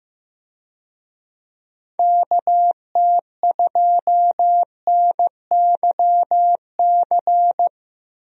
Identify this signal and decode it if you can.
Morse code, heard as a keyed tone, about 16 wpm, 700 Hz. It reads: KT2NYC
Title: USA Call Signs - 6 Characters - 15wpm